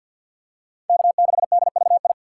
Morse code, heard as a keyed tone, about 47 wpm, 700 Hz. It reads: K6DVI
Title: USA Call Signs - 6x Character Spacing - 50wpm